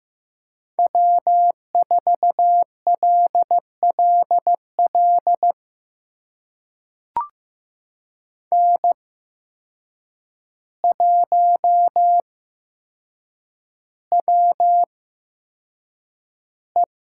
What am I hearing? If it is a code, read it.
Morse code, heard as a keyed tone, about 16 wpm, spaced as for 7 wpm, 700 Hz. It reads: W4LLL N1WE
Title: USA Call Signs - 5 Characters - 8x Character Spacing - 15wpm